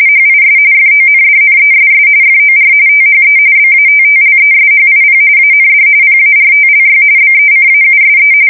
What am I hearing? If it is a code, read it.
RTTY, about 45.45 baud. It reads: HIS FONDNESS FOR THE BATTLESHIP AND PARTLY BECAUSE
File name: fsk170 / USS-Missouri